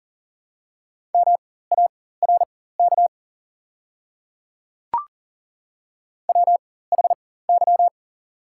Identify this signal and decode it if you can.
Morse code, heard as a keyed tone, about 40 wpm, spaced as for 19 wpm, 690 Hz. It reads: MARK WHY
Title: 4x Character Spacing - Top 300 Words - 40wpm